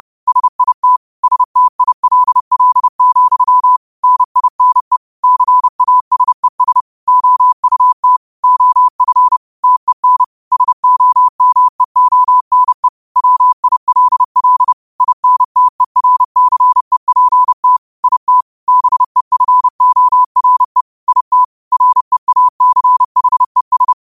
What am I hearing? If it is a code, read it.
Morse code, heard as a keyed tone, about 30 wpm, 1000 Hz. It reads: SIT STILL, NINE CASES OUT OF TEN SOMEONE WILL INTERCEPT IT BEFORE IT REACHES
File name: qotdfiles / 30wpm